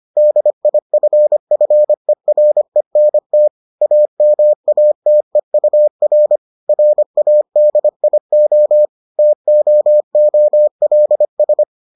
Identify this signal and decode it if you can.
Morse code, heard as a keyed tone, about 25 wpm, 600 Hz. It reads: DIFFERENT AMATEUR RADIO TOOLS